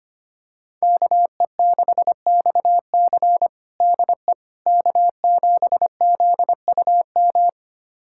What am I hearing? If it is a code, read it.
Morse code, heard as a keyed tone, about 25 wpm, 700 Hz. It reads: KE6XC DE K7ZUM
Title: USA Callsign DE Callsign - 5 Characters - 25wpm